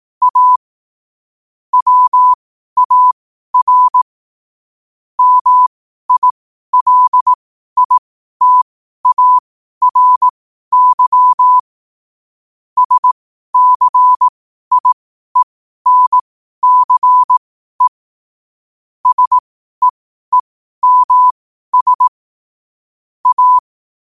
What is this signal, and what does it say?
Morse code, heard as a keyed tone, about 18 wpm, spaced as for 12 wpm, 1000 Hz. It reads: A WAR MILITARY SCIENCE SEEMS A
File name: qotdfiles / 12wpm